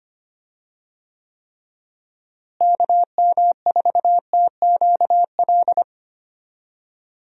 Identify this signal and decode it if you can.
Morse code, heard as a keyed tone, about 25 wpm, 700 Hz. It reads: KM4TQL
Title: USA Call Signs - 6 Characters - Repeated 3x - 25wpm